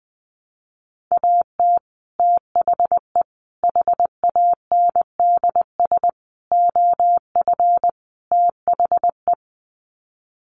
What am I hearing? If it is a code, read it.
Morse code, heard as a keyed tone, about 20 wpm, 700 Hz. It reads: AT THE HANDS OF THE
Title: Sets of 5 Words Encore2 20wpm